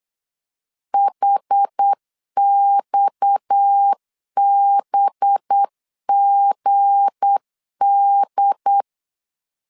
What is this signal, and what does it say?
Morse code, heard as a keyed tone, about 8 wpm, 800 Hz. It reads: HXBGD
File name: CW / DET / 1 9 1500C S C 8wpm 5653